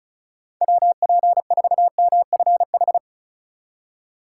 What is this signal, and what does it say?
Morse code, heard as a keyed tone, about 36 wpm, 700 Hz. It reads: WP4MFH
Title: USA Call Signs - 6 Characters - 3x Character Spacing - 35wpm